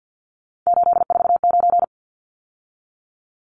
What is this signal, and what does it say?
Morse code, heard as a keyed tone, about 50 wpm, 700 Hz. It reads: 849
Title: Numbers - 3-Digit 50wpm